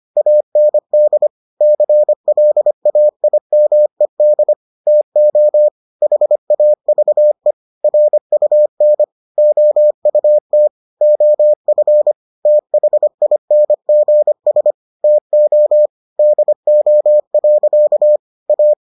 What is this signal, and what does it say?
Morse code, heard as a keyed tone, about 25 wpm, 600 Hz. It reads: AND CLAIMED TO HAVE RUN OUT OF THINGS TO DO. A